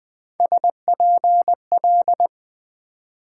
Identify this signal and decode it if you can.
Morse code, heard as a keyed tone, about 26 wpm, 700 Hz. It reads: SPL